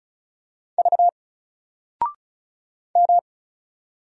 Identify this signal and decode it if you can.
Morse code, heard as a keyed tone, about 34 wpm, 700 Hz. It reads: V M
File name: Single-Letter-Number-35wpm